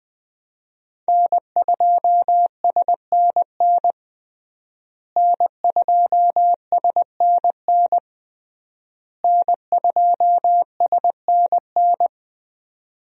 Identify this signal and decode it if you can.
Morse code, heard as a keyed tone, about 21 wpm, 700 Hz. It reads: N2SNN N2SNN N2SNN
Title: USA Call Signs - 5 Characters - Repeated 3x - 20wpm